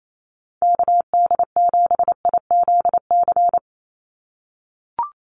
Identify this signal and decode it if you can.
Morse code, heard as a keyed tone, about 28 wpm, 700 Hz. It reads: KD7IZC
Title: USA Call Signs 28wpm